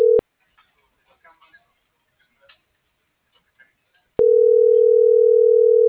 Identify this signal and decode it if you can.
Ringback tone (North America), 440 + 480 Hz.